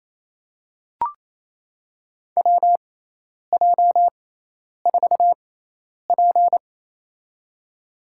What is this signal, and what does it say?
Morse code, heard as a keyed tone, about 28 wpm, spaced as for 10 wpm, 700 Hz. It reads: WJ4P
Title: USA Call Signs - 4 Characters - 6x Character Spacing - 28wpm